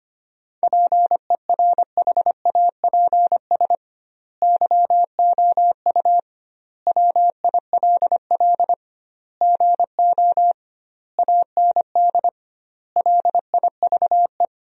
Morse code, heard as a keyed tone, about 25 wpm, 700 Hz. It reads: PERHAPS YOU WILL GO AND LIVE
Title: 2x Word-Spacing - Sentences from Top 500 Words - 25wpm